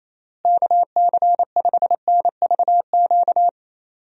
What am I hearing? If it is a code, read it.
Morse code, heard as a keyed tone, about 28 wpm, 700 Hz. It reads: KC5NVQ
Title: USA Call Signs - 6 Characters - Repeated 2x - 28wpm